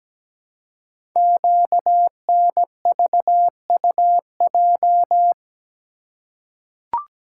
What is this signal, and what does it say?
Morse code, heard as a keyed tone, about 17 wpm, 700 Hz. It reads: QNVUJ
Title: Code Groups - 5 Letters - 17wpm